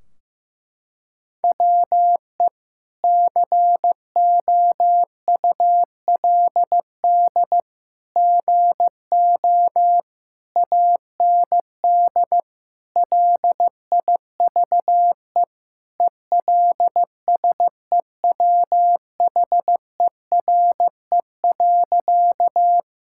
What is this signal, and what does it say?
Morse code, heard as a keyed tone, about 15 wpm, 700 Hz. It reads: WE COULD GO AND LIVE ELSEWHERE.
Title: Sentences from Top 1600 Words 15wpm